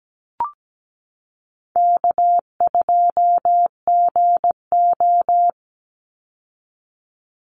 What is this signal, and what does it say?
Morse code, heard as a keyed tone, about 17 wpm, 700 Hz. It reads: K2GO
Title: Call Signs Repeated 2x - 17wpm